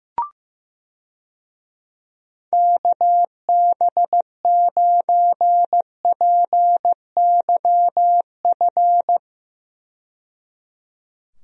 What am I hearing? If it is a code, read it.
Morse code, heard as a keyed tone, about 15 wpm, 700 Hz. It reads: KB9PYF
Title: USA Call Signs - 6 Characters - 15wpm